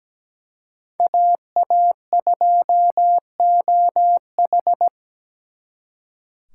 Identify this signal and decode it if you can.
Morse code, heard as a keyed tone, about 20 wpm, 700 Hz. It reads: AA2OH
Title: Speed Racing - USA 2x2 Call Signs 17wpm